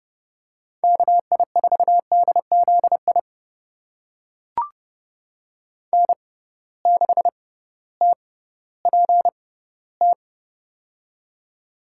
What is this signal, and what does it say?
Morse code, heard as a keyed tone, about 30 wpm, spaced as for 16 wpm, 700 Hz. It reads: KI4DZI N6TPT
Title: USA Call Signs - 6x Character Spacing - 30wpm